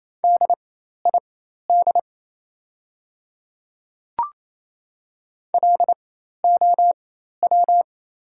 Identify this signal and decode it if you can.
Morse code, heard as a keyed tone, about 28 wpm, spaced as for 13 wpm, 700 Hz. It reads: DID LOW
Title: 4x Character Spacing - Top 200 Words - 28wpm